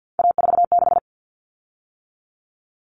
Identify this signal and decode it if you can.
Morse code, heard as a keyed tone, about 50 wpm, 700 Hz. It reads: A46
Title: Common Cut Numbers - 3-Digit 50wpm